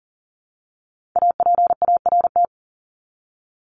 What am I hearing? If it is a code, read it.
Morse code, heard as a keyed tone, about 40 wpm, 700 Hz. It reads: APART
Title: Top 1501-1750 Words 40wpm